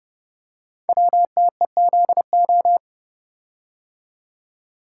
Morse code, heard as a keyed tone, about 29 wpm, 690 Hz. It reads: WTEZO
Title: Code Groups - 5 Letters - 30wpm